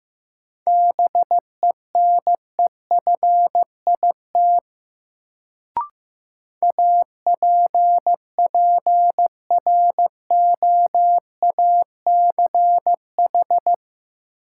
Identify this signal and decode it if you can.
Morse code, heard as a keyed tone, about 15 wpm, 700 Hz. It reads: BENEFIT APPROACH
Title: Sentences from Top 1000 Words - Review Part 6 - New Word Drill - 15wpm